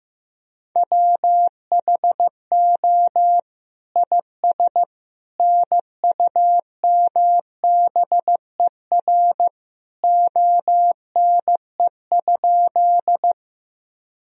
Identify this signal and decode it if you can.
Morse code, heard as a keyed tone, about 15 wpm, 700 Hz. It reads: WHO IS NUMBER ONE?
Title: Sentences from Top 1000 Words - Review Part 1 - 15wpm